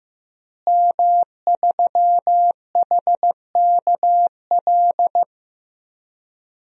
Morse code, heard as a keyed tone, about 15 wpm, 700 Hz. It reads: M3HKL
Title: DXCC-100 Call Signs - 5 Characters - Repeated 2x - 15wpm